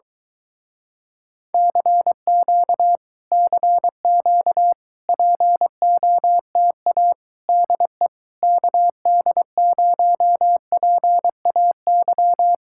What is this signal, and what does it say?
Morse code, heard as a keyed tone, about 23 wpm, 700 Hz. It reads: CQ CQ POTA DE KD0PAY